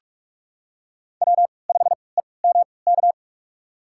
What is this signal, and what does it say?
Morse code, heard as a keyed tone, about 43 wpm, spaced as for 25 wpm, 700 Hz. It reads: W5EMK
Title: USA Call Signs - 5 Characters - 3x Character Spacing - 45wpm